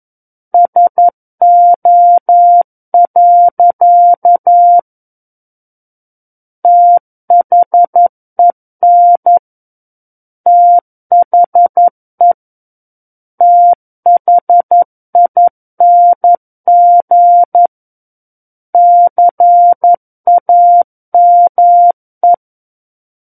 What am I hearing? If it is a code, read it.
Morse code, heard as a keyed tone, about 11 wpm, 700 Hz. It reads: SO. THEN THE THING CAME
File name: Book One / 02-Chapter 2 - 11 WPM